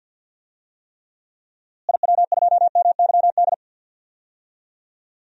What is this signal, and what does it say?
Morse code, heard as a keyed tone, about 51 wpm, 700 Hz. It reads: IW2MYD
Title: DXCC-100 Call Signs - 6 Characters - Repeated 2x - 50wpm